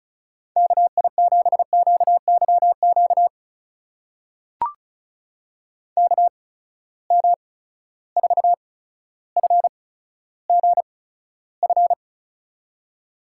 Morse code, heard as a keyed tone, about 36 wpm, spaced as for 15 wpm, 700 Hz. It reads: KI7QYQ KM4FGF
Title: USA Call Signs - 6 Characters - 8x Character Spacing - 35wpm